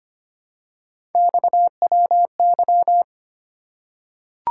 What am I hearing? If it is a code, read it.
Morse code, heard as a keyed tone, about 25 wpm, 700 Hz. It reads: XWY
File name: Three Letter-Number Combinations - 25wpm